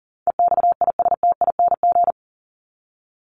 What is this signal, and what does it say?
Morse code, heard as a keyed tone, about 40 wpm, 700 Hz. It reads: EXISTING
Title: ING Suffix - Top 100 Words with Base Words 40wpm